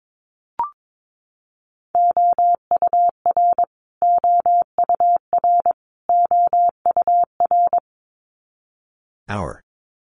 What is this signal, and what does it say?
Morse code, heard as a keyed tone, about 22 wpm, 700 Hz. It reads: OUR OUR OUR
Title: Top 500 Words Repeated 22wpm